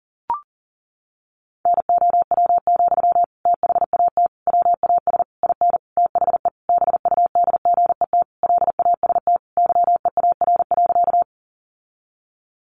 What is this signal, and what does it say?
Morse code, heard as a keyed tone, about 40 wpm, 700 Hz. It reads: NOW, THAT WAS IN THE BUDGET LAST YEAR.
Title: Sentences from Top 1200 Words 40wpm